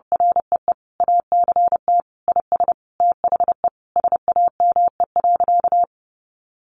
Morse code, heard as a keyed tone, about 30 wpm, 700 Hz. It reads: REE ACT IS THE SAME.
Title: Sentences from Top 1400 Words 30wpm